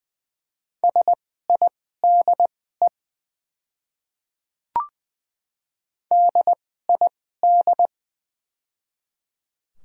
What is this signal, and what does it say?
Morse code, heard as a keyed tone, about 21 wpm, spaced as for 15 wpm, 690 Hz. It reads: SIDE DID
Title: 2x Character Spacing - Top 100 Words - 20wpm